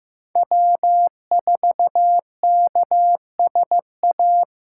Morse code, heard as a keyed tone, about 18 wpm, 700 Hz. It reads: W4KSA
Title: Speed Racing - USA 1x3 Call Signs 15wpm